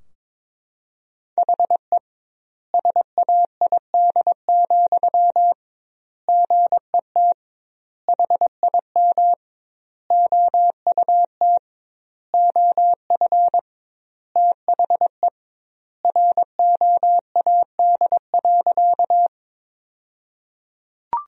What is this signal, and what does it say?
Morse code, heard as a keyed tone, about 22 wpm, 700 Hz. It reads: HE SAID, GET HIM OUT OF THE ROAD.
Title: 2x Word-Spacing - Sentences from Top 400 Words - 22wpm